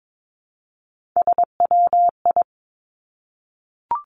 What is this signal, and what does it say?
Morse code, heard as a keyed tone, about 22 wpm, 700 Hz. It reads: SWI
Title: Three Challenging Letter-Number Combinations 22wpm